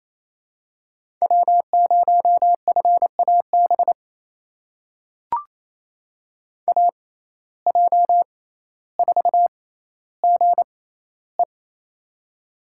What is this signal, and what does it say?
Morse code, heard as a keyed tone, about 28 wpm, spaced as for 14 wpm, 700 Hz. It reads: W0FAB AJ4GE
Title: USA Call Signs - 5 Characters - 6x Character Spacing - 28wpm